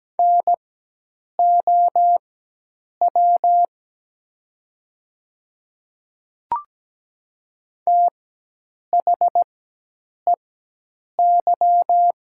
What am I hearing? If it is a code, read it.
Morse code, heard as a keyed tone, about 17 wpm, spaced as for 8 wpm, 700 Hz. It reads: NOW THEY
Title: 4x Character Spacing - Top 100 Words - 17wpm